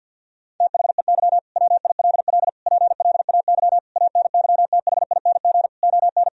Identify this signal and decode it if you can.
Morse code, heard as a keyed tone, about 50 wpm, 700 Hz. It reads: THEY WILL PLAY ANYTHING ON
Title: Sentences from Top 1000 Words - Review Part 4 - 50wpm